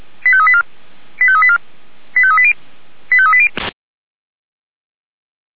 Five-tone selective call (ZVEI), heard as quick runs of five tones, tones of about 70 ms. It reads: 85384 85384 85380 85380
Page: SDR/ZVEI